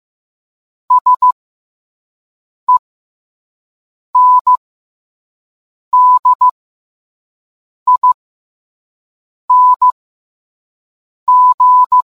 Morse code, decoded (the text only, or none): SENDING